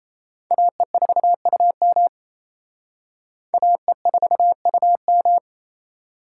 Morse code, heard as a keyed tone, about 37 wpm, 700 Hz. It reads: AE4UM AE4UM